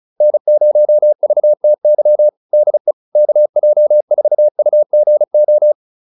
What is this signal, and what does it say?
Morse code, heard as a keyed tone, about 34 wpm, 600 Hz. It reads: N0VTY DE KJ4UGO